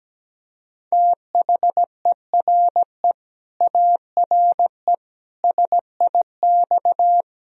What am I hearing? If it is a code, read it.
Morse code, heard as a keyed tone, about 17 wpm, 700 Hz. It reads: THERE ARE SIX
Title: Sentences from Top 1000 Words - Review Part 2 - 17wpm